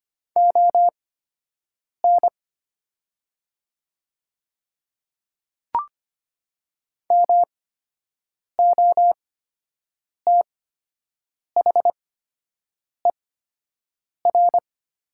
Morse code, heard as a keyed tone, about 25 wpm, spaced as for 7 wpm, 700 Hz. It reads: ON MOTHER
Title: 8x Character Spacing - Top 300 Words - 25wpm